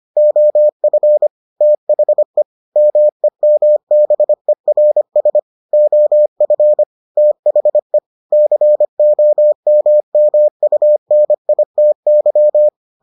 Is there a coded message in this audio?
Yes: Morse code from 0.2 to 12.7 s, heard as a keyed tone, about 25 wpm, 600 Hz. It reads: OF THE MEMBERS OF THE COMMUNITY